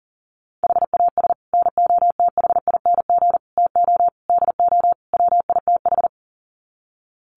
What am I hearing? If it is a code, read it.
Morse code, heard as a keyed tone, about 40 wpm, 700 Hz. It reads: HAS NOTHING TO DO WITH
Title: Sets of 5 Words 40wpm